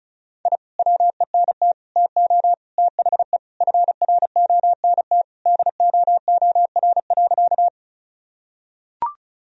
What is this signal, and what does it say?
Morse code, heard as a keyed tone, about 36 wpm, 700 Hz. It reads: I WENT TO THE FRONT DOOR.
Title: Sentences from Top 1000 Words - Review Part 2 - 35wpm